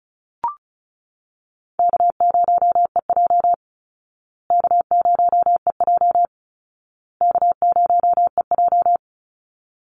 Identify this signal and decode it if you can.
Morse code, heard as a keyed tone, about 34 wpm, 700 Hz. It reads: K0EJ K0EJ K0EJ
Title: Call Signs Repeated 2x - 35wpm